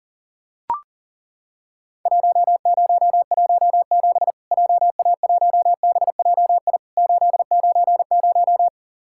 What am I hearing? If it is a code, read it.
Morse code, heard as a keyed tone, about 40 wpm, 700 Hz. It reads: 1017 JA1BJI 890
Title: Worked All Europe DX Contest - QTC Exchange - 40wpm